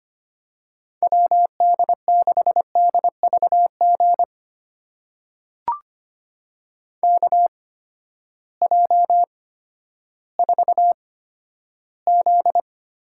Morse code, heard as a keyed tone, about 25 wpm, spaced as for 12 wpm, 700 Hz. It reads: WD6DVG KJ4Z
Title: USA Call Signs - 6 Characters - 8x Character Spacing - 25wpm